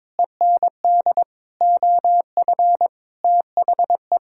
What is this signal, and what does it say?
Morse code, heard as a keyed tone, about 22 wpm, 700 Hz. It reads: END OF THE
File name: practice / 22wpm / Sets of 8 Words - 22wpm